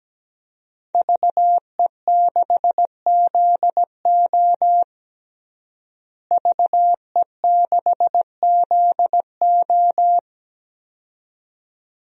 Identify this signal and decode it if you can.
Morse code, heard as a keyed tone, about 17 wpm, 700 Hz. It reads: VE6ZO VE6ZO
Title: DXCC-100 Call Signs - 5 Characters - Repeated 2x - 17wpm